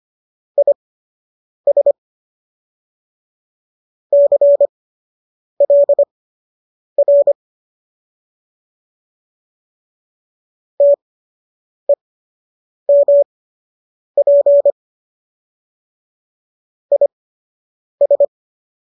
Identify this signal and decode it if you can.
Morse code, heard as a keyed tone, about 25 wpm, spaced as for 8 wpm, 570 Hz. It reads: IS CLR TEMP IS
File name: qso003 08